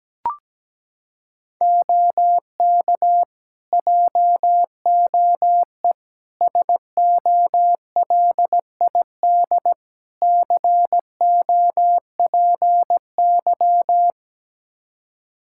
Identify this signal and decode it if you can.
Morse code, heard as a keyed tone, about 17 wpm, 700 Hz. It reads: OK JOE SOLID COPY
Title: Practice QSO - Exchange 3 - More Info - 17wpm